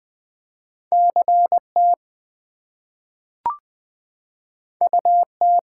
Morse code, heard as a keyed tone, about 20 wpm, 700 Hz. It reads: CT UT
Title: US States and Canadian Provinces 20wpm